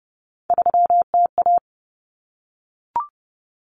Morse code, heard as a keyed tone, about 30 wpm, 700 Hz. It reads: 3TA
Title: Common Cut Numbers - 3-Digit 30wpm